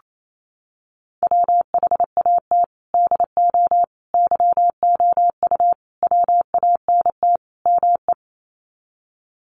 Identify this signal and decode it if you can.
Morse code, heard as a keyed tone, about 28 wpm, 700 Hz. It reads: WHAT DO YOU WANT ME
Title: Sets of 5 Words Encore2 28wpm